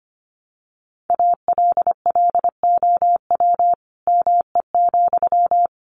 Morse code, heard as a keyed tone, about 25 wpm, 700 Hz. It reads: ALLOW ME,
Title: Sentences from Top 700 Words 25wpm